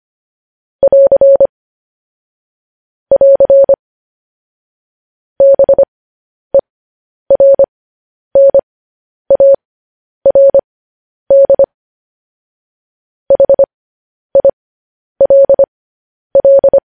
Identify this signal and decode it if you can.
Morse code, heard as a keyed tone, about 25 wpm, spaced as for 10 wpm, 550 Hz. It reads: <AR> <AR> BERNARD HILL